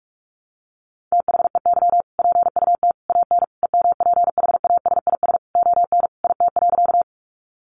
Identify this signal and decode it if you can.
Morse code, heard as a keyed tone, about 44 wpm, 700 Hz. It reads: THEY PUT AN EMPHASIS ON IT.
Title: Sentences from Top 1400 Words 45wpm